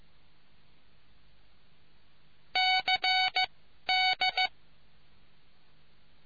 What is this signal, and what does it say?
Morse code, heard as a keyed tone, about 15 wpm, spaced as for 11 wpm, 750 Hz. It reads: CD